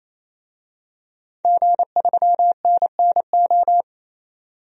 Morse code, heard as a keyed tone, about 28 wpm, 700 Hz. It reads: G3NNO
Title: DXCC-100 Call Signs - 5 Characters - 28wpm